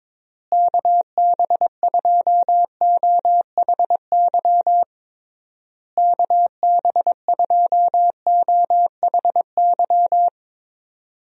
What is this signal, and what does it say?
Morse code, heard as a keyed tone, about 22 wpm, 700 Hz. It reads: KB2OHY KB2OHY
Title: USA Call Signs - 6 Characters - Repeated 3x - 22wpm